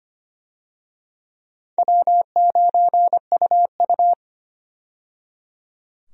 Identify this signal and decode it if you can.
Morse code, heard as a keyed tone, about 25 wpm, 700 Hz. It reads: W9UU
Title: USA Call Signs - 4 Characters - 25wpm